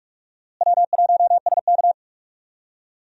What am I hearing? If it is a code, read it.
Morse code, heard as a keyed tone, about 44 wpm, 700 Hz. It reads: W1SK
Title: USA Call Signs - 4 Characters - Repeated 3x - 45wpm